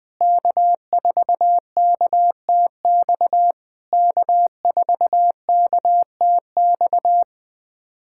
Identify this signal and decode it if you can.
Morse code, heard as a keyed tone, about 20 wpm, 700 Hz. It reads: K4KTX K4KTX
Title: SOTA QSO Practice 20wpm